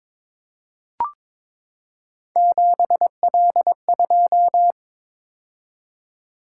Morse code, heard as a keyed tone, about 22 wpm, 700 Hz. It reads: 7L2
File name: practice / 22wpm / Three Letter-Number Combinations - 22wpm